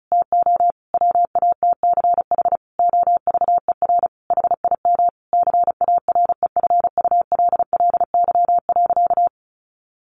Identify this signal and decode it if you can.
Morse code, heard as a keyed tone, about 34 wpm, 700 Hz. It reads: TO WATCH OVER HIM CAREFULLY.
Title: Sentences from Top 1500 Words 35wpm